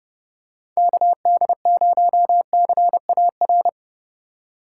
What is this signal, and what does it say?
Morse code, heard as a keyed tone, about 30 wpm, 700 Hz. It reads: KD0CAR
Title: USA Call Signs - 4x Character Spacing - 30wpm